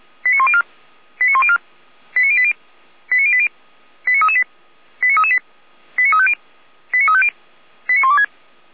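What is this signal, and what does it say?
Five-tone selective call (ZVEI), heard as quick runs of five tones, tones of about 70 ms. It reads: 89104 89104 89080 89080 89338 89338 89350 89350 89126
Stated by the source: ZVEI five-tone selective call received by SDR, from the feuerwehr-orlamuende fire brigade audio archive.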